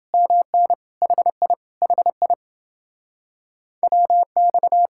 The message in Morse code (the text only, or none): MN HI HI WX